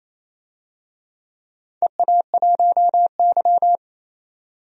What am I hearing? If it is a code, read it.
Morse code, heard as a keyed tone, about 27 wpm, 700 Hz. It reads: EA1Y
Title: DXCC-100 Call Signs - 4 Characters - 28wpm